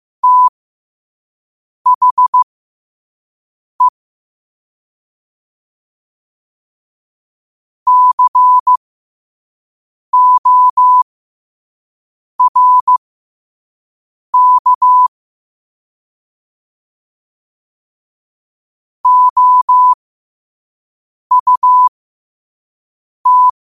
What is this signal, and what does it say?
Morse code, heard as a keyed tone, about 15 wpm, spaced as for 5 wpm, 1000 Hz. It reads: THE CORK OUT
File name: qotdfiles / 5wpm